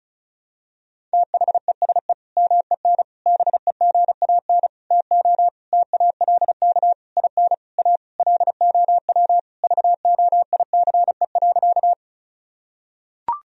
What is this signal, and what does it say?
Morse code, heard as a keyed tone, about 36 wpm, 700 Hz. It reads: THESE MEN BEGAN TO TALK IN A LOW VOICE.
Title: Sentences from Top 1000 Words - Review Part 1 - 35wpm